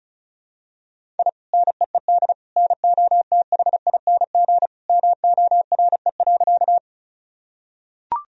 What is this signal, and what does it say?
Morse code, heard as a keyed tone, about 34 wpm, 700 Hz. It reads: I NEED NOTHING MORE.
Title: Sentences from Top 1000 Words - Review Part 1 - 35wpm